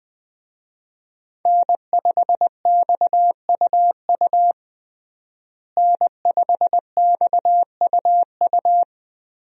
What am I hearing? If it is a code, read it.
Morse code, heard as a keyed tone, about 21 wpm, 700 Hz. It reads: N5XUU N5XUU
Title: USA Call Signs - 5 Characters - Repeated 2x - 20wpm